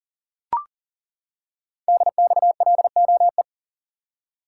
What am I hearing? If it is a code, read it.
Morse code, heard as a keyed tone, about 40 wpm, 700 Hz. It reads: DXLOE
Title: Code Groups - 5 Letters - 40wpm